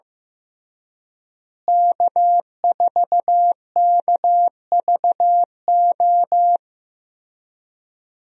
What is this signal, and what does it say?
Morse code, heard as a keyed tone, about 15 wpm, 700 Hz. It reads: K4KVO
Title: USA Call Signs - 5 Characters - Repeated 3x - 15wpm